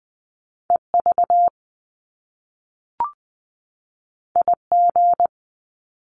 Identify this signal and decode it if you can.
Morse code, heard as a keyed tone, about 20 wpm, 700 Hz. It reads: E V I G
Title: Top Two Letter Combinations in Words 20wpm